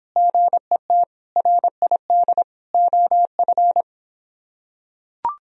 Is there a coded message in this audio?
Yes: Morse code from 0.2 to 3.8 s, heard as a keyed tone, about 24 wpm, 700 Hz. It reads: GET RID OF